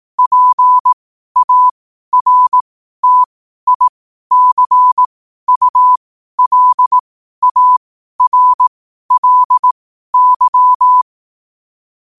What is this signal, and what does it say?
Morse code, heard as a keyed tone, about 18 wpm, spaced as for 12 wpm, 1000 Hz. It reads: PARTICULARLY
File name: qotdfiles / 12wpm